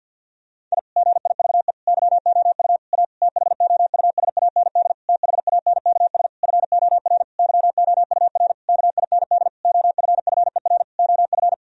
Morse code, heard as a keyed tone, about 51 wpm, 700 Hz. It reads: I GIVE YOU A THOUSAND THANKS FOR YOUR KIND OFFER OF
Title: Sentences from Top 1000 Words - Review Part 4 - 50wpm